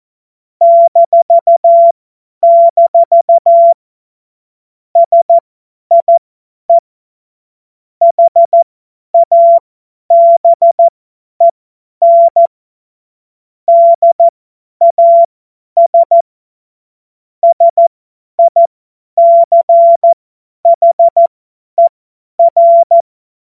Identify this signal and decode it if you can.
Morse code, heard as a keyed tone, about 14 wpm, spaced as for 10 wpm, 680 Hz. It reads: -- SIE HABEN DAS SICHER